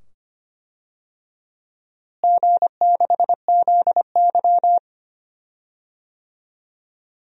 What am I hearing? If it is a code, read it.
Morse code, heard as a keyed tone, about 25 wpm, 700 Hz. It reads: G6ZY
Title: DXCC-100 Call Signs - 4 Characters - Repeated 2x - 25wpm